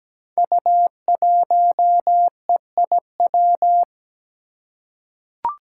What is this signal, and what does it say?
Morse code, heard as a keyed tone, about 17 wpm, 700 Hz. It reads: U1EIW
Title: Five Challenging Letter-Number Combinations 17wpm